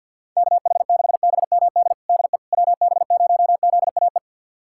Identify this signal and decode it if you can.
Morse code, heard as a keyed tone, about 50 wpm, 690 Hz. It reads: KH6BMD DE WB0ZAE
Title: USA Callsign DE Callsign - 6 Characters - 50wpm